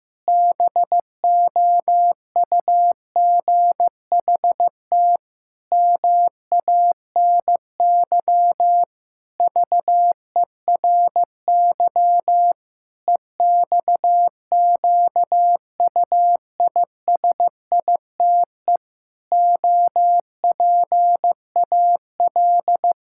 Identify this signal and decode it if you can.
Morse code, heard as a keyed tone, about 15 wpm, 700 Hz. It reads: BOUGHT MANY VERY EXQUISITE OPAL
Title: Pangram Challenge v2 - 15wpm